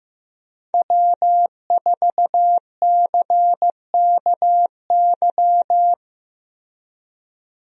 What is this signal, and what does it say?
Morse code, heard as a keyed tone, about 15 wpm, 700 Hz. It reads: W4CKY
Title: USA Call Signs - 5 Characters - Repeated 3x - 15wpm